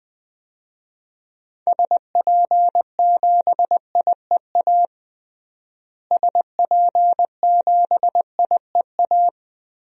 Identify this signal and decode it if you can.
Morse code, heard as a keyed tone, about 20 wpm, 700 Hz. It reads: SP7IEA SP7IEA
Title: DXCC-100 Call Signs - 6 Characters - Repeated 3x - 20wpm